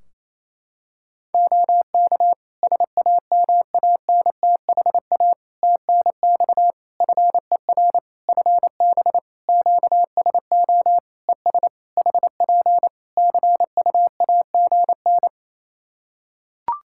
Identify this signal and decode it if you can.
Morse code, heard as a keyed tone, about 28 wpm, 700 Hz. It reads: OK SAMANTHA TNX FER FB QSO ES HP CUAGN
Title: Practice QSO - Exchange 4 - Ending - 28wpm